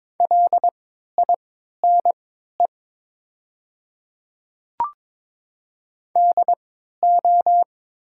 Morse code, heard as a keyed tone, about 22 wpm, spaced as for 12 wpm, 700 Hz. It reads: LINE DO
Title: 3x Character Spacing - Top 300 Words - 22wpm